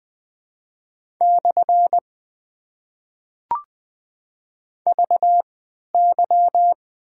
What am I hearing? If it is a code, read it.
Morse code, heard as a keyed tone, about 20 wpm, spaced as for 11 wpm, 690 Hz. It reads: / VY
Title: Top 90 QSO Elements - 3x Character Spacing - 20wpm